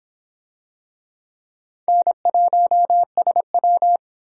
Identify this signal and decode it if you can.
Morse code, heard as a keyed tone, about 25 wpm, 700 Hz. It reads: N1SW